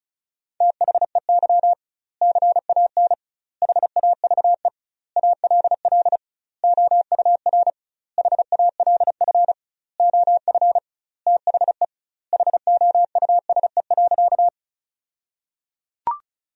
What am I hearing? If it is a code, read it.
Morse code, heard as a keyed tone, about 36 wpm, 700 Hz. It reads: THEY CAN HAVE ALL OUR HALF OF THE HOUSE.
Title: 2x Word-Spacing - Sentences from Top 400 Words - 35wpm